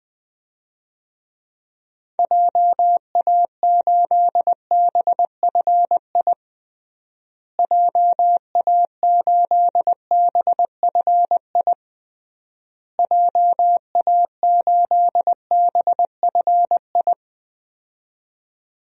Morse code, heard as a keyed tone, about 20 wpm, 700 Hz. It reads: JA8BFI JA8BFI JA8BFI
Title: DXCC-100 Call Signs - 6 Characters - Repeated 3x - 20wpm